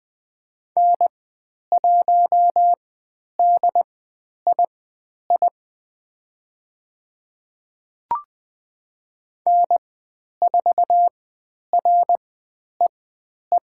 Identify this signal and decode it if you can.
Morse code, heard as a keyed tone, about 20 wpm, spaced as for 10 wpm, 700 Hz. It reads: N1DII N4REE
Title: 095 Lesson 16g - Letter c - Call Signs 20wpm